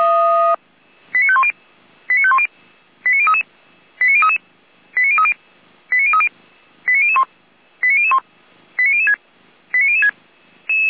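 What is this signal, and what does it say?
Five-tone selective call (ZVEI), heard as quick runs of five tones, tones of about 70 ms. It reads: 89510 89510 89033 89033 89030 89030 89001 89001 89006 89006
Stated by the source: ZVEI five-tone call received via SDR, archived by feuerwehr-orlamuende.org fire brigade.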